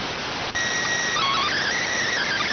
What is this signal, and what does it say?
SSTV (slow-scan television), PD 290.